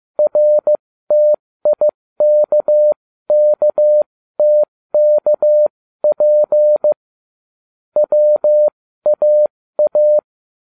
Morse code, heard as a keyed tone, about 15 wpm, spaced as for 13 wpm, 600 Hz. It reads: RTIKKTKP WAA